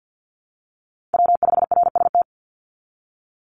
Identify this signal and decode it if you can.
Morse code, heard as a keyed tone, about 50 wpm, 700 Hz. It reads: W5RST
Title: NA SOTA Chaser Callsigns 50wpm